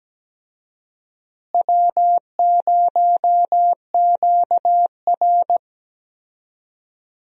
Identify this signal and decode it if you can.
Morse code, heard as a keyed tone, about 17 wpm, 700 Hz. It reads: W0QR
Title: USA Call Signs - 4 Characters - Repeated 2x - 17wpm